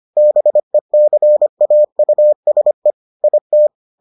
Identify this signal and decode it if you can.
Morse code, heard as a keyed tone, about 25 wpm, 600 Hz. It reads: BECAUSE IT